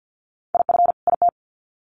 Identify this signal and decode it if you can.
Morse code, heard as a keyed tone, about 50 wpm, 700 Hz. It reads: IF IT
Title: Sets of 2 Words 50wpm